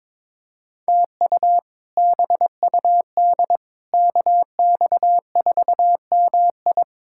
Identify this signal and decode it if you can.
Morse code, heard as a keyed tone, about 22 wpm, 700 Hz. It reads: TU BUD KX4MI
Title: K1USN SST QSO Practice - 22wpm